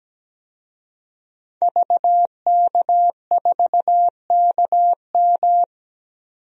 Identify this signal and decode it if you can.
Morse code, heard as a keyed tone, about 17 wpm, 700 Hz. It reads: VK4KM
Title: DXCC-100 Call Signs - 5 Characters - 17wpm